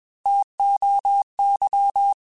Morse code, heard as a keyed tone, about 21 wpm, 800 Hz. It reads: TOY
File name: T Runs / T Runs mp3s / T Runs @20